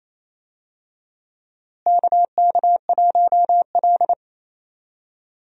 Morse code, heard as a keyed tone, about 28 wpm, 700 Hz. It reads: KK1L
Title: USA Call Signs - 4 Characters - Repeated 2x - 28wpm